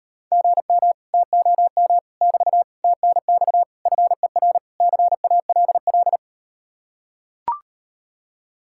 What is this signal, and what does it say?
Morse code, heard as a keyed tone, about 38 wpm, 700 Hz. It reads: GM TOM <BT> TNX FER CALL